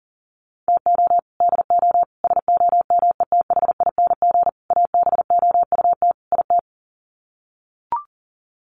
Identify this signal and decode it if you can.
Morse code, heard as a keyed tone, about 40 wpm, 700 Hz. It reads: TO DO SOMETHING ABOUT IT
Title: Sets of 5 Words 40wpm